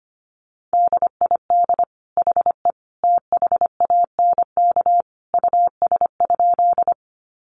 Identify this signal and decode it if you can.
Morse code, heard as a keyed tone, about 25 wpm, 700 Hz. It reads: DID HE THANK US?
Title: Sentences from Top 600 Words 25wpm